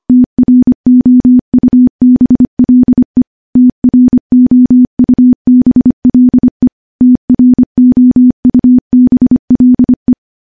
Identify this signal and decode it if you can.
Morse code, heard as a keyed tone, about 25 wpm, 250 Hz. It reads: TROUBLE TROUBLE TROUBLE
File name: morse-training-25WPM